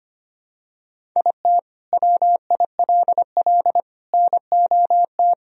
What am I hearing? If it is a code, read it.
Morse code, heard as a keyed tone, about 25 wpm, 700 Hz. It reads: IT WILL NOT
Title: Sentences from Top 1000 Words - Review Part 1 - 25wpm